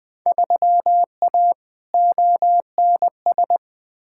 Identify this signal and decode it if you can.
Morse code, heard as a keyed tone, about 21 wpm, 700 Hz. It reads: 3A ONS
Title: Field Day Exchanges - 20wpm